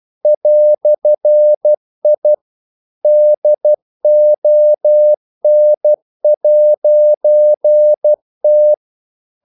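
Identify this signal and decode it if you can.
Morse code, heard as a keyed tone, about 12 wpm, 600 Hz. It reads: "I DON'T
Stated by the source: Morse practice sending English quotations.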